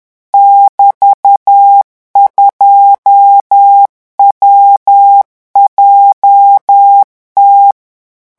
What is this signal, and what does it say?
Morse code, heard as a keyed tone, about 10 wpm, 800 Hz. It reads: <BT>2WJT
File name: morse mp3 / revision17-vitesse 10 mots